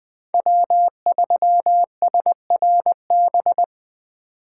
Morse code, heard as a keyed tone, about 21 wpm, 700 Hz. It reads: W3SRB
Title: 2023 POTA Activator Call Signs - 20wpm